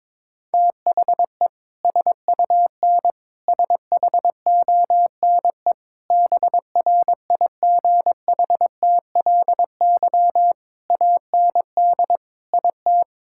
Morse code, heard as a keyed tone, about 22 wpm, 700 Hz. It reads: THE SUN SHONE BRIGHTLY AND IT
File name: practice / 22wpm / Sentences with 2-Word Phrases - 22wpm